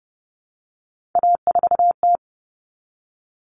Morse code, heard as a keyed tone, about 30 wpm, 700 Hz. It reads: A4T
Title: All Cut Numbers - 3-Digit 30wpm